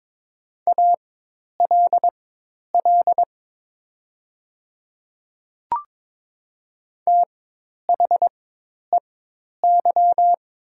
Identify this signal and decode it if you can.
Morse code, heard as a keyed tone, about 22 wpm, spaced as for 10 wpm, 700 Hz. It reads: ALL THEY
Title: 4x Character Spacing - Top 100 Words - 22wpm